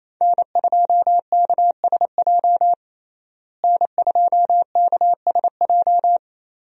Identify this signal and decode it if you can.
Morse code, heard as a keyed tone, about 28 wpm, 700 Hz. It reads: N2KSJ N2KSJ
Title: USA Call Signs - 5 Characters - Repeated 3x - 28wpm